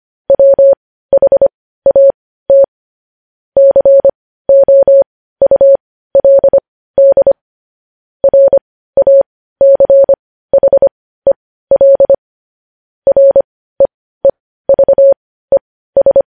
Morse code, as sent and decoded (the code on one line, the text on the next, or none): .-- .... .- - / -.-. --- ..- .-.. -.. / .-. .- -.-. .... .-.. / .-. . ...- . ...
WHAT COULD RACHEL REEVES